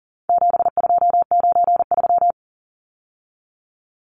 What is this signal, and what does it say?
Morse code, heard as a keyed tone, about 40 wpm, 700 Hz. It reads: 7293
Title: Numbers - 4-Digit 40wpm